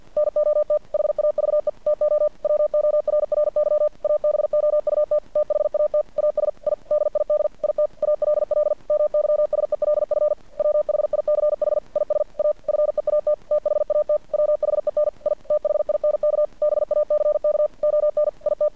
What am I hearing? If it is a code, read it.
Morse code, heard as a keyed tone, about 50 wpm, 600 Hz. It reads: NOT HAVE TO WORRY ABOUT THAT AS I DID IT ALL MYSELF WHICH IS A FEAT THAT WHEN I THINK BACK ON IT
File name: archived / CRSnet